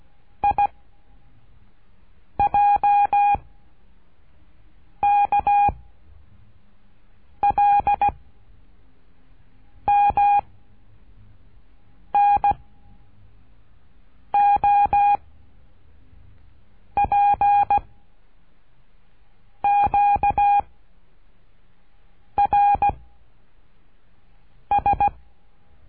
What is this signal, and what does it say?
Morse code, heard as a keyed tone, about 17 wpm, 810 Hz. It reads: I J K L M N O P Q R S
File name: cursos / CURSO DE TELEGRAFIA / Telegrafia - Trilha 13